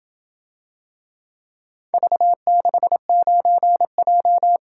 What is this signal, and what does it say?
Morse code, heard as a keyed tone, about 27 wpm, 700 Hz. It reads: V69J